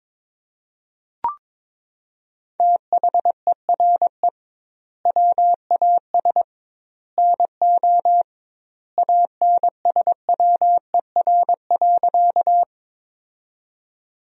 Morse code, heard as a keyed tone, about 22 wpm, 700 Hz. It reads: THERE WAS NO ANSWER.
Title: 2x Word-Spacing - Sentences from Top 300 Words - 22wpm